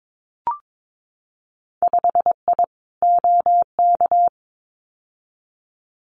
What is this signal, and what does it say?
Morse code, heard as a keyed tone, about 22 wpm, 700 Hz. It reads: 5I OK
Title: Winter Field Day Exchanges Encore 22wpm